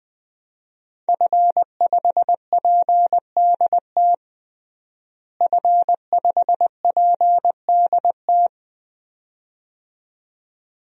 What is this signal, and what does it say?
Morse code, heard as a keyed tone, about 20 wpm, 700 Hz. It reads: F5PDT F5PDT
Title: DXCC-100 Call Signs - 5 Characters - Repeated 2x - 20wpm